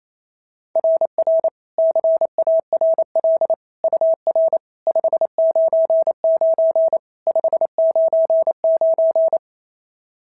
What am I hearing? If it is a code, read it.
Morse code, heard as a keyed tone, about 27 wpm, 650 Hz. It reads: RR CARL UR 599 599
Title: SOTA QSO Practice 28wpm